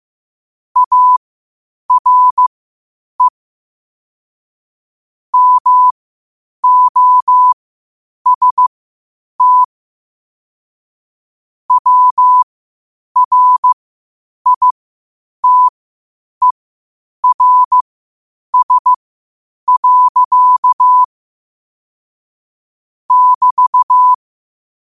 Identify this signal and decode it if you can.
Morse code, heard as a keyed tone, about 15 wpm, spaced as for 8 wpm, 1000 Hz. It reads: ARE MOST WRITERS. <BT>
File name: qotdfiles / 8wpm